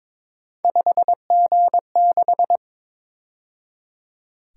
Morse code, heard as a keyed tone, about 26 wpm, 700 Hz. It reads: 5G6
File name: Speed-Racing - Three Letter-Number Combinations - 22wpm